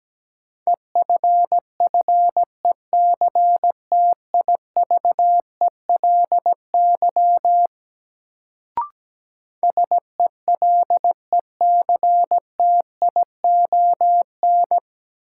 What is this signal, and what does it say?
Morse code, heard as a keyed tone, about 17 wpm, 700 Hz. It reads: EFFECTIVELY SELECTION
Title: Sentences from Top 1400 Words - New Word Drill - 17wpm